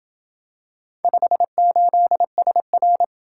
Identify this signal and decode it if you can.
Morse code, heard as a keyed tone, about 27 wpm, 700 Hz. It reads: 58SR